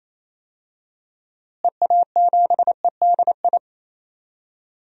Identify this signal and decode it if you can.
Morse code, heard as a keyed tone, about 28 wpm, 700 Hz. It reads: EA7EDI
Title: DXCC-100 Call Signs - 6 Characters - Repeated 2x - 28wpm